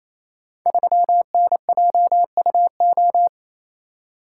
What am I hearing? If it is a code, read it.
Morse code, heard as a keyed tone, about 28 wpm, 700 Hz. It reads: 3NJUO
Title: Code Groups - 5 Letter:Numbers - 28wpm